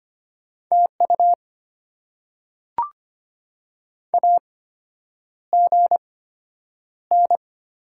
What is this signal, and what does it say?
Morse code, heard as a keyed tone, about 25 wpm, spaced as for 9 wpm, 700 Hz. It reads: TU AGN
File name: Top 90 QSO Elements - 8x Character Spacing - 25wpm